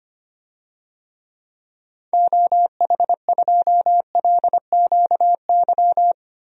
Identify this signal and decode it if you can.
Morse code, heard as a keyed tone, about 25 wpm, 700 Hz. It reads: OH2LQY
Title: DXCC-100 Call Signs - 6 Characters - 25wpm